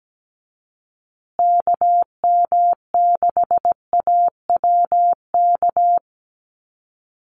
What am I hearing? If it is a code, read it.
Morse code, heard as a keyed tone, about 17 wpm, 700 Hz. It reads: KM6AWK
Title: USA Call Signs Repeated 2x 17wpm